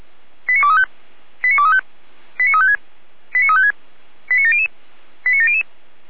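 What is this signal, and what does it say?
Five-tone selective call (ZVEI), heard as quick runs of five tones, tones of about 70 ms. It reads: 89235 89235 89356 89356 89700 89700
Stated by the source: ZVEI five-tone selective call to address 89126, received via SDR.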